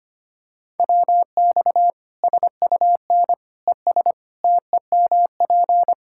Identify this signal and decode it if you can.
Morse code, heard as a keyed tone, about 25 wpm, 700 Hz. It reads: WX SUN ES TEMP
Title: K1USN SST QSO Practice 25wpm